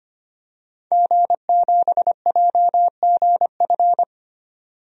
Morse code, heard as a keyed tone, about 25 wpm, 700 Hz. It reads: G7JGF
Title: DXCC-100 Call Signs - 5 Characters - 25wpm